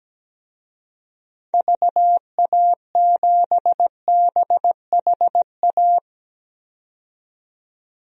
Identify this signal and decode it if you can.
Morse code, heard as a keyed tone, about 17 wpm, 700 Hz. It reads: VA7BHA
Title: DXCC-100 Call Signs - 6 Characters - 17wpm